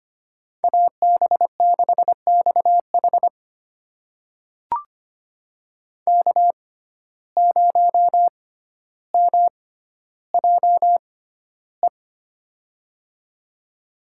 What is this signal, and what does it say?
Morse code, heard as a keyed tone, about 25 wpm, spaced as for 13 wpm, 700 Hz. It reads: AB6XH K0MJE
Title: USA Call Signs - 5 Characters - 6x Character Spacing - 25wpm